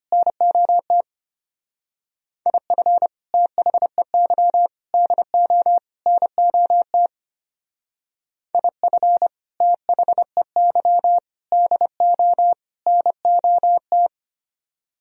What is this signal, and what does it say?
Morse code, heard as a keyed tone, about 30 wpm, 700 Hz. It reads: NOT IF THEY DO NOT IF THEY DO NOT
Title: Speed-Racing - Sets of 4 Words 25wpm